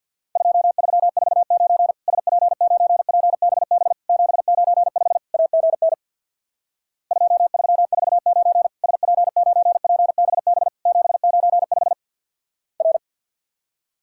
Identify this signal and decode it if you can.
Morse code, heard as a keyed tone, about 50 wpm, 700 Hz. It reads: 2349 SP9PBB 795 AGN 2349 SP9PBB 795 R
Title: Worked All Europe DX Contest - QTC Exchange with Repeats - 50wpm